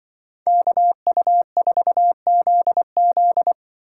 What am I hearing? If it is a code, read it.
Morse code, heard as a keyed tone, about 24 wpm, 700 Hz. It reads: KU4ZZ
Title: Speed Racing - USA Call Signs 20wpm